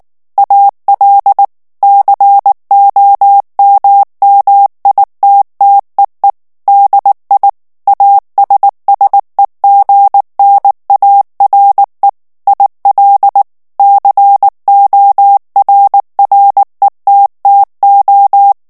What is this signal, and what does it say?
Morse code, heard as a keyed tone, about 19 wpm, 800 Hz. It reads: AL COMMITTEE DI ASSEGNARE IL CORRETTO
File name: morse mp3 / ITALIEN texte clair 2 vitesse 18 mots minute